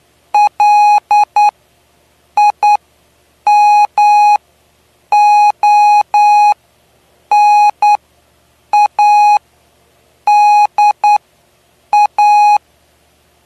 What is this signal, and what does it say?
Morse code, heard as a keyed tone, about 10 wpm, spaced as for 7 wpm, 830 Hz. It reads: LIMONADA
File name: cw / py2lcd / 09-PY2LCD-Li��o 9